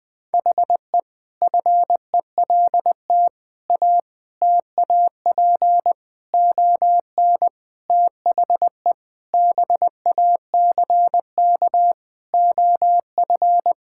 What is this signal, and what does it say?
Morse code, heard as a keyed tone, about 21 wpm, 700 Hz. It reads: HE FELT A TAP ON THE BACK OF
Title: Sentences with 4-Word Phrases - Spoken Beginning - 20wpm